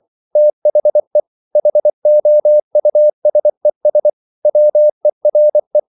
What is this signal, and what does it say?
Morse code, heard as a keyed tone, about 24 wpm, 600 Hz. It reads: THE HOUSES WERE